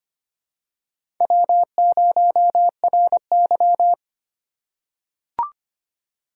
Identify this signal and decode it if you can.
Morse code, heard as a keyed tone, about 25 wpm, 700 Hz. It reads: W0RY
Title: USA Call Signs - 4 Characters - 2x Character Spacing - 25wpm